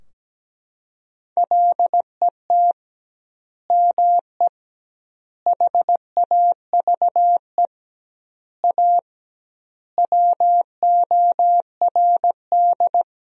Morse code, heard as a keyed tone, about 17 wpm, 700 Hz. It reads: LET ME HAVE A WORD
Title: 2x Word-Spacing - Sentences from Top 300 Words - 17wpm